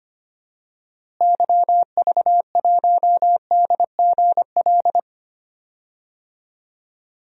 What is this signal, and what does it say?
Morse code, heard as a keyed tone, about 25 wpm, 700 Hz. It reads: YV1DGL
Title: DXCC-100 Call Signs - 6 Characters - Repeated 3x - 25wpm